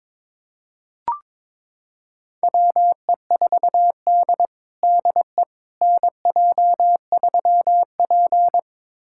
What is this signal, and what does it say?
Morse code, heard as a keyed tone, about 21 wpm, 700 Hz. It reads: WE4D DE NJ3P
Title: USA Callsign DE Callsign - 4 Characters - 22wpm